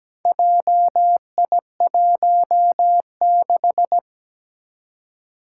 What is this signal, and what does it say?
Morse code, heard as a keyed tone, about 17 wpm, 700 Hz. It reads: JI16
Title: Four Challenging Letter-Number Combinations 17wpm